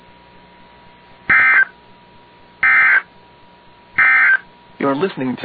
Emergency Alert System (SAME) tones, heard as harsh buzzing bursts. It is heard from 1.2 to 4.5 s.